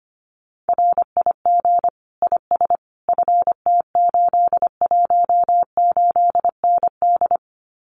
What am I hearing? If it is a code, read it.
Morse code, heard as a keyed tone, about 25 wpm, 700 Hz. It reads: RIG IS FT818ND
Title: Common QSO Phrases Encore 25wpm